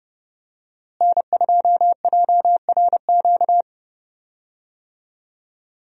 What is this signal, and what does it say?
Morse code, heard as a keyed tone, about 30 wpm, 700 Hz. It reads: N2JRQ
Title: USA Call Signs - 5 Characters - Repeated 3x - 30wpm